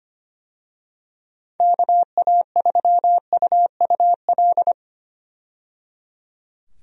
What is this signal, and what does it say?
Morse code, heard as a keyed tone, about 25 wpm, 700 Hz. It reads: KA3UUL
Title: USA Call Signs - 6 Characters - 25wpm